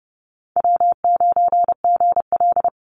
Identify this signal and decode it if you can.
Morse code, heard as a keyed tone, about 30 wpm, 700 Hz. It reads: W9GL
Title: Call Signs - 30wpm